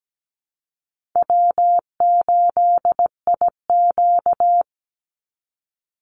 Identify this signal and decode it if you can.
Morse code, heard as a keyed tone, about 17 wpm, 700 Hz. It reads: W8IQ
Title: Easy Call Signs - 17wpm